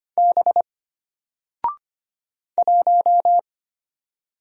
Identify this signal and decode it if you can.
Morse code, heard as a keyed tone, about 25 wpm, 700 Hz. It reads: B 1
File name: Single-Letter-Number-25wpm